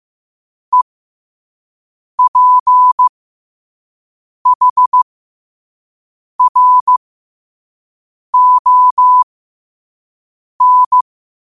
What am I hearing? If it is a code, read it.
Morse code, heard as a keyed tone, about 15 wpm, spaced as for 5 wpm, 1000 Hz. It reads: EPHRON